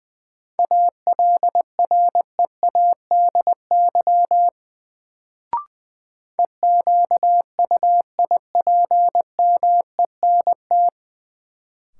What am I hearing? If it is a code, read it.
Morse code, heard as a keyed tone, about 20 wpm, 700 Hz. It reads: ALREADY EQUIPMENT
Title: Sentences from Top 700 Words - New Word Drill - 20wpm